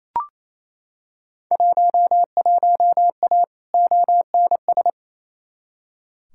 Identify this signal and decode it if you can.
Morse code, heard as a keyed tone, about 28 wpm, 700 Hz. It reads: 11A ONS
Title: Field Day Exchanges Encore - 28wpm